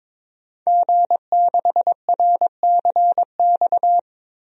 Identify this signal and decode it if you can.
Morse code, heard as a keyed tone, about 22 wpm, 700 Hz. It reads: G6RCX
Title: DXCC-100 Call Signs - 5 Characters - Repeated 3x - 22wpm